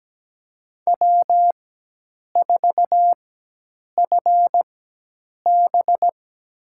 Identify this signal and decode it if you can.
Morse code, heard as a keyed tone, about 17 wpm, spaced as for 8 wpm, 700 Hz. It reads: W4FB
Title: USA Call Signs - 4 Characters - 4x Character Spacing - 17wpm